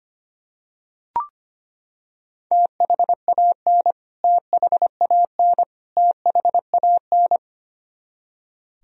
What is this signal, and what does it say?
Morse code, heard as a keyed tone, about 25 wpm, 700 Hz. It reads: THAN THAN THAN
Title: Top 100 Words Repeated 25wpm